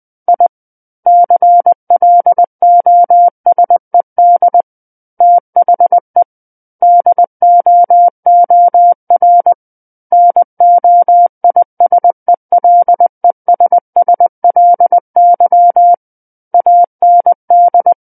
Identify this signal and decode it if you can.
Morse code, heard as a keyed tone, about 20 wpm, 700 Hz. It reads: I CLOSED THE DOOR NOISELESSLY AND
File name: 20 WPM